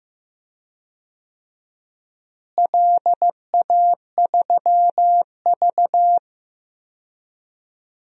Morse code, heard as a keyed tone, about 15 wpm, 700 Hz. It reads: LA3V
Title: DXCC-100 Call Signs - 4 Characters - Repeated 3x - 15wpm